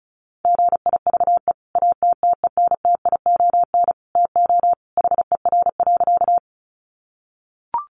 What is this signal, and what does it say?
Morse code, heard as a keyed tone, about 34 wpm, 700 Hz. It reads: GIVE ATTENTION TO HER.
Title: Sentences from Top 1000 Words 35wpm